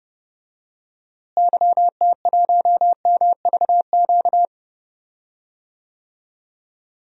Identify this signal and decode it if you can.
Morse code, heard as a keyed tone, about 30 wpm, 700 Hz. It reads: YT1MVQ
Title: DXCC-100 Call Signs - 6 Characters - Repeated 3x - 30wpm